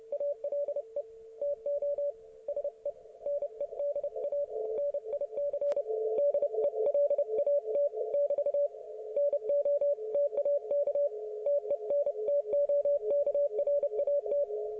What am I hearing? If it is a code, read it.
Morse code, heard as a keyed tone, about 30 wpm, 570 Hz. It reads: ALE TO SE NEDA NIC DELAT <BT> NO TAK TENTOKRAT
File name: ok1fhi